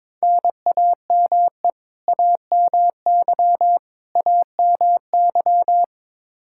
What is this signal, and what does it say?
Morse code, heard as a keyed tone, about 22 wpm, 700 Hz. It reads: NAME AMY AMY
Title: Practice QSO - Exchange 1 - Beginning - 22wpm